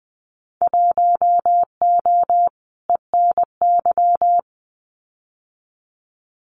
Morse code, heard as a keyed tone, about 20 wpm, 700 Hz. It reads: 1O ENY
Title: Winter Field Day Exchanges 20wpm